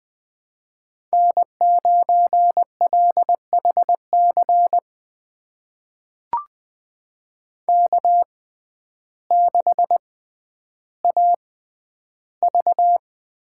Morse code, heard as a keyed tone, about 20 wpm, spaced as for 11 wpm, 700 Hz. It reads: N9LHC K6AV
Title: USA Call Signs - 6x Character Spacing - 20wpm